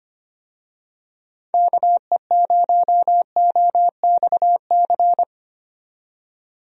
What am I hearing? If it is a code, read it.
Morse code, heard as a keyed tone, about 25 wpm, 700 Hz. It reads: KE0OXC
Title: USA Call Signs - 6 Characters - 25wpm